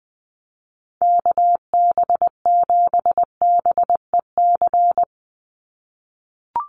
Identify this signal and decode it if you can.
Morse code, heard as a keyed tone, about 20 wpm, 700 Hz. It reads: KB7BEC
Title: USA Call Signs 20wpm